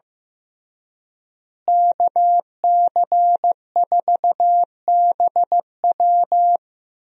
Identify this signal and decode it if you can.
Morse code, heard as a keyed tone, about 15 wpm, 700 Hz. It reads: KC4BW
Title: USA Call Signs - 5 Characters - Repeated 3x - 15wpm